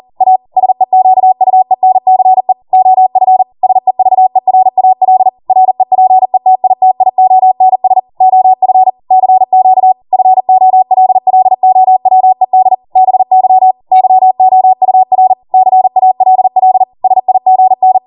Morse code, heard as a keyed tone, about 40 wpm, 750 Hz. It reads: A SEQUENCE OF SEVERAL REPETITIONS OF CQ FOLLOWED BY YOUR CALL SIGN